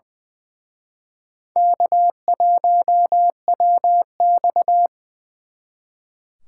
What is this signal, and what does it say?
Morse code, heard as a keyed tone, about 25 wpm, 700 Hz. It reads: K1WX
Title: Speed Racing - USA 1x2 Call Signs 20wpm